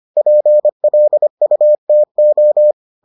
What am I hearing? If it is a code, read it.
Morse code, heard as a keyed tone, about 25 wpm, 600 Hz. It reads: PLUTO